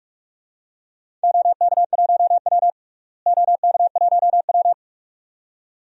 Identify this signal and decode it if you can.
Morse code, heard as a keyed tone, about 46 wpm, 700 Hz. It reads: OK1W OK1W
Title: DXCC-100 Call Signs - 4 Characters - Repeated 2x - 45wpm